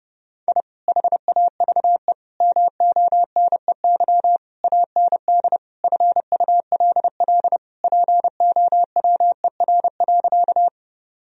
Morse code, heard as a keyed tone, about 32 wpm, 700 Hz. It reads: I HAVE MONEY AND FULL POWER.
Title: Sentences from Top 1000 Words - Review Part 3 - 30wpm